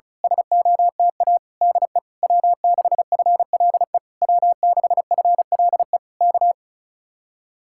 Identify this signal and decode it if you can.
Morse code, heard as a keyed tone, about 32 wpm, 700 Hz. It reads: SOTA DE W6FLE W6FLE K
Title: Speed-Racing Common QSO Phrases Encore - 35wpm